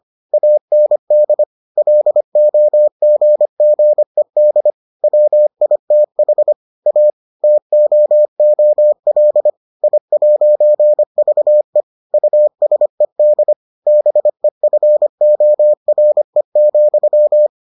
Morse code, heard as a keyed tone, about 25 wpm, 600 Hz. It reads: AND LOGGED WITH A TOOL I'VE USED BEFORE,